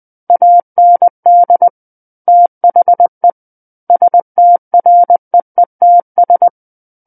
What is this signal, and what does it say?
Morse code, heard as a keyed tone, about 20 wpm, 700 Hz. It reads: AND THE STREETS
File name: 20 WPM